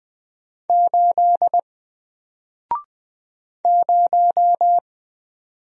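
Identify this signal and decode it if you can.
Morse code, heard as a keyed tone, about 20 wpm, 700 Hz. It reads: 8 0
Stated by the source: Single letter and number Morse drill at 20 wpm, from N4LCD.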